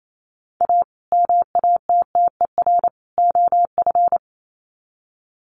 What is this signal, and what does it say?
Morse code, heard as a keyed tone, about 28 wpm, 700 Hz. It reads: A MATTER OF
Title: Sets of 3 Words 28wpm